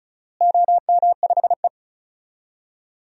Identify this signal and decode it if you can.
Morse code, heard as a keyed tone, about 36 wpm, 700 Hz. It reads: OM5E
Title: DXCC-100 Call Signs - 4 Characters - Repeated 2x - 35wpm